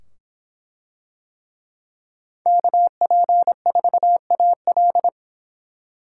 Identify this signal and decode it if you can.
Morse code, heard as a keyed tone, about 23 wpm, 700 Hz. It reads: KP4AL